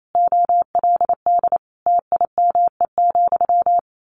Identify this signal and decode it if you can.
Morse code, heard as a keyed tone, about 28 wpm, 700 Hz. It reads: OLD TIME,
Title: A Sound Of Thunder 28wpm